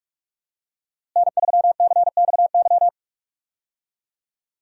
Morse code, heard as a keyed tone, about 43 wpm, 700 Hz. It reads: N3CXY
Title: USA Call Signs - 5 Characters - Repeated 3x - 45wpm